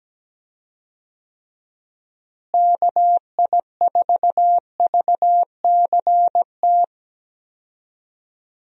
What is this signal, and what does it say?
Morse code, heard as a keyed tone, about 17 wpm, 700 Hz. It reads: KI4VCT
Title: USA Call Signs - 6 Characters - Repeated 3x - 17wpm